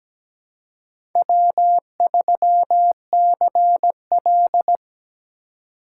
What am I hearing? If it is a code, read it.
Morse code, heard as a keyed tone, about 17 wpm, 700 Hz. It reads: W3CL
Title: USA Call Signs - 4 Characters - 3x Character Spacing - 17wpm